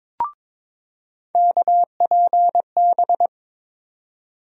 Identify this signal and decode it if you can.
Morse code, heard as a keyed tone, about 22 wpm, 700 Hz. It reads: KPB
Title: USA Call Sign Postfixes - 3 Letters 22wpm